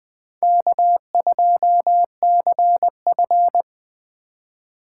Morse code, heard as a keyed tone, about 20 wpm, 700 Hz. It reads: K2CF
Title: USA Call Signs - 4 Characters - 6x Character Spacing - 20wpm